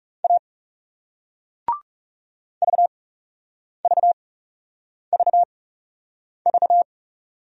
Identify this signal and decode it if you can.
Morse code, heard as a keyed tone, about 39 wpm, spaced as for 8 wpm, 700 Hz. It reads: A VVVV